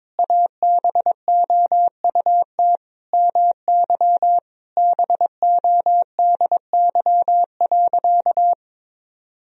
Morse code, heard as a keyed tone, about 22 wpm, 700 Hz. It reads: ABOUT MY BODY.
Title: Sentences from Top 1000 Words - Review Part 2 - 22wpm